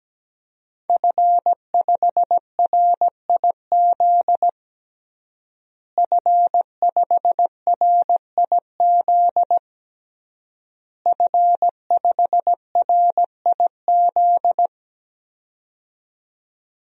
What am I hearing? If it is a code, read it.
Morse code, heard as a keyed tone, about 17 wpm, 700 Hz. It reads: F5RIZ F5RIZ F5RIZ
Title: DXCC-100 Call Signs - 5 Characters - Repeated 3x - 17wpm